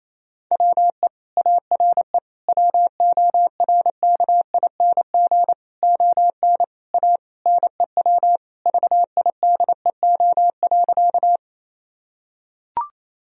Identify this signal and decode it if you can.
Morse code, heard as a keyed tone, about 28 wpm, 700 Hz. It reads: WE ARE WORKING ON A NEW VIDEO.
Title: Sentences from Top 1000 Words - Review Part 4 - 28wpm